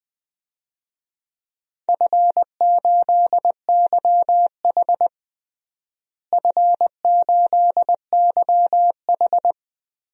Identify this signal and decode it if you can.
Morse code, heard as a keyed tone, about 21 wpm, 700 Hz. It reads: F8YH F8YH
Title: DXCC-100 Call Signs - 4 Characters - Repeated 3x - 20wpm